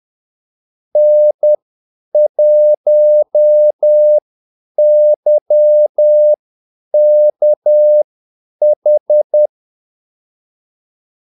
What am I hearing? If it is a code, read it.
Morse code, heard as a keyed tone, about 10 wpm, spaced as for 8 wpm, 600 Hz. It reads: N1YKH
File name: Callsigns-10-8wpm